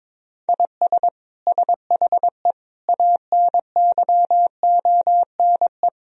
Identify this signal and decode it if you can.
Morse code, heard as a keyed tone, about 22 wpm, 700 Hz. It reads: IS SHE ANYONE
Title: Sentences from Top 1000 Words - Review Part 4 - 22wpm